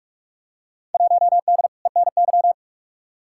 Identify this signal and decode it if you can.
Morse code, heard as a keyed tone, about 44 wpm, 700 Hz. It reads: 1D ENY
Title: Field Day Exchanges - 45wpm